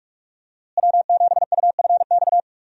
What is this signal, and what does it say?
Morse code, heard as a keyed tone, about 42 wpm, 700 Hz. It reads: W7UFX
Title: USA Call Signs - 5 Characters - Repeated 2x - 45wpm